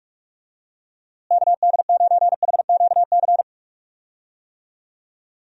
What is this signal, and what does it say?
Morse code, heard as a keyed tone, about 41 wpm, 700 Hz. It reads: KD9HQC
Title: USA Call Signs - 6 Characters - Repeated 3x - 45wpm